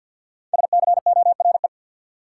Morse code, heard as a keyed tone, about 52 wpm, 700 Hz. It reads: SCORE
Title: Sentences from Top 1600 Words - New Word Drill Alt - 50wpm